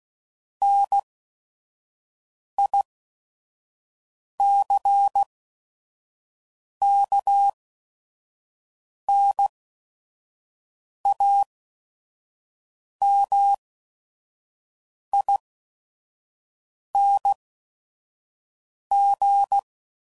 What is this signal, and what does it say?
Morse code, heard as a keyed tone, about 16 wpm, spaced as for 5 wpm, 800 Hz. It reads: NICKNAMING
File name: CW Code Groups @5WPM / Technical Subjects / AND Gate